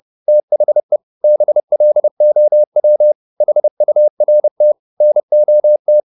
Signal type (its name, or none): Morse code